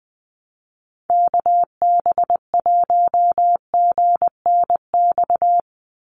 Morse code, heard as a keyed tone, about 20 wpm, 700 Hz. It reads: KB1GNX
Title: USA Call Signs 20wpm